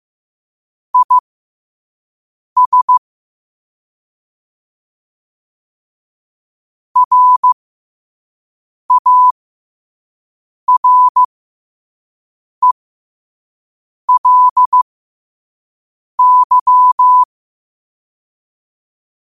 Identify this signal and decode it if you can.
Morse code, heard as a keyed tone, about 15 wpm, spaced as for 5 wpm, 1000 Hz. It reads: IS RARELY
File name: qotdfiles / 5wpm